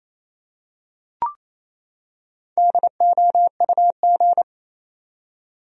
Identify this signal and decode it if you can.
Morse code, heard as a keyed tone, about 28 wpm, 700 Hz. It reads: DOUG
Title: USA Common Names - Max 5 Letters - 28wpm